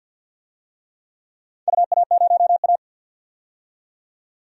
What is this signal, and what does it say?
Morse code, heard as a keyed tone, about 46 wpm, 700 Hz. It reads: UA0A
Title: DXCC-100 Call Signs - 4 Characters - 50wpm